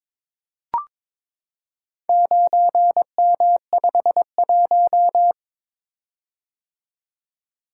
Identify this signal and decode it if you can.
Morse code, heard as a keyed tone, about 22 wpm, 700 Hz. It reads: 9M51
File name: practice / 22wpm / Four Letter-Number Combinations - 22wpm